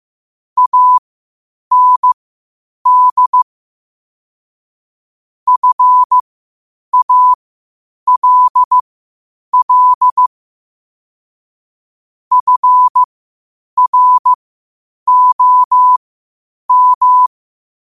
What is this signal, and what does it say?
Morse code, heard as a keyed tone, about 15 wpm, spaced as for 8 wpm, 1000 Hz. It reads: AND FALL FROM
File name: qotdfiles / 8wpm